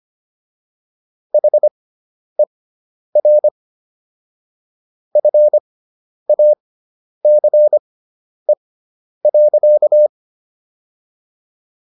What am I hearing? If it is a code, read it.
Morse code, heard as a keyed tone, about 25 wpm, spaced as for 10 wpm, 600 Hz. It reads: HER FACE.